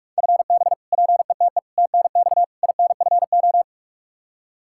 Morse code, heard as a keyed tone, about 42 wpm, 700 Hz. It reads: FB PETE TNX INFO
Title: Practice QSO - Exchange 3 - More Info - 45wpm